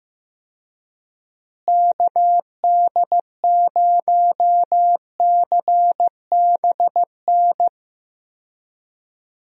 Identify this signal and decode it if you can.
Morse code, heard as a keyed tone, about 15 wpm, 700 Hz. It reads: KD0CBN
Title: USA Call Signs - 6 Characters - Repeated 3x - 15wpm